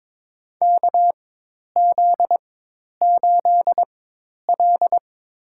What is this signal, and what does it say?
Morse code, heard as a keyed tone, about 22 wpm, spaced as for 10 wpm, 700 Hz. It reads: KZ8L
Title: USA Call Signs - 4x Character Spacing - 22wpm